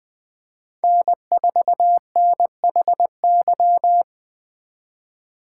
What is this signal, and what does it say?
Morse code, heard as a keyed tone, about 20 wpm, 700 Hz. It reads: N4NHY
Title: Speed Racing - USA 1x3 Call Signs 17wpm